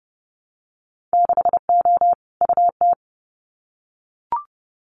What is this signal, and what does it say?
Morse code, heard as a keyed tone, about 30 wpm, 700 Hz. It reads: 6O UT
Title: Winter Field Day Exchanges Encore 30wpm